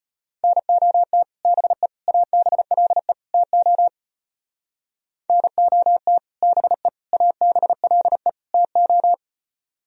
Morse code, heard as a keyed tone, about 31 wpm, 700 Hz. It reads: NOT BE ABLE TO NOT BE ABLE TO
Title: Speed-Racing - Sets of 4 Words 25wpm